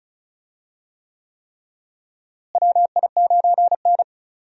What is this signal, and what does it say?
Morse code, heard as a keyed tone, about 36 wpm, 700 Hz. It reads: WI9N
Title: USA Call Signs - 4 Characters - Repeated 3x - 35wpm